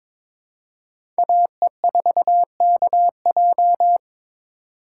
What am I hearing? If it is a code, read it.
Morse code, heard as a keyed tone, about 22 wpm, 700 Hz. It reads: AE4KJ
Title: USA Call Signs - 5 Characters - 3x Character Spacing - 22wpm